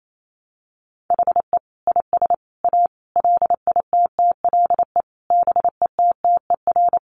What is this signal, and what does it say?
Morse code, heard as a keyed tone, about 28 wpm, 700 Hz. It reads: HE IS A LITTLE BETTER
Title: Sentences from Top 600 Words 28wpm